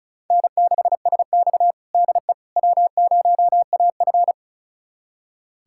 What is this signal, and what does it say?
Morse code, heard as a keyed tone, about 34 wpm, 700 Hz. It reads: N6SX DE W0AF
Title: USA Callsign DE Callsign - 4 Characters - 35wpm